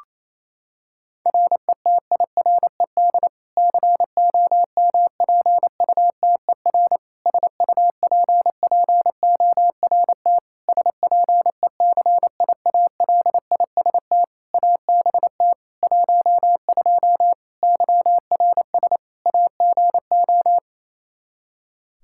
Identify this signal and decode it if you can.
Morse code, heard as a keyed tone, about 28 wpm, 700 Hz. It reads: RETIRED COMPUTER SUPPORT SPECIALIST ABT 12 YRS AGO
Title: Practice QSO - Exchange 3 - More Info - 28wpm